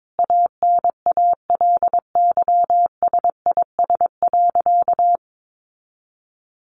Morse code, heard as a keyed tone, about 22 wpm, 700 Hz. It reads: ANALYSIS.
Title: Sentences from Top 1400 Words 22wpm